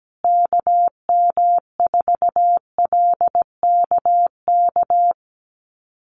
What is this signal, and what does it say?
Morse code, heard as a keyed tone, about 17 wpm, 700 Hz. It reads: KM4LKK
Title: USA Call Signs Repeated 2x 17wpm